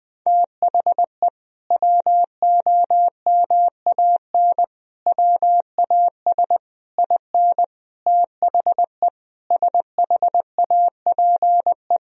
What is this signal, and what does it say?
Morse code, heard as a keyed tone, about 21 wpm, 700 Hz. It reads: THE WOMAN WAS IN THE SHAPE
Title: Sentences from Top 1000 Words - Review Part 3 - 20wpm